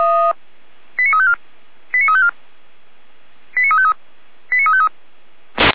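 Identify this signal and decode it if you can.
Five-tone selective call (ZVEI), heard as quick runs of five tones, tones of about 70 ms. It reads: 89354 89354 89353 89353
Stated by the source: ZVEI five-tone selective call received on an SDR.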